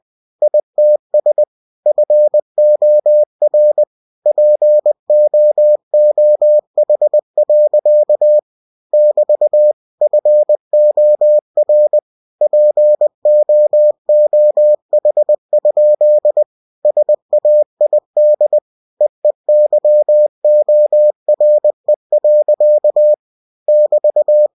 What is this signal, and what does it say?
Morse code, heard as a keyed tone, about 20 wpm, 600 Hz. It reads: ITS FOR POOH. <BT> FOR POOH? SAID EEYORE. <BT>
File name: cw-books / winnie-the-pooh / winnie-20wpm-page-0108